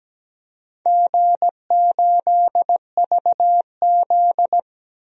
Morse code, heard as a keyed tone, about 17 wpm, 700 Hz. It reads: G8VZ
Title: DXCC-100 Call Signs - 4 Characters - 17wpm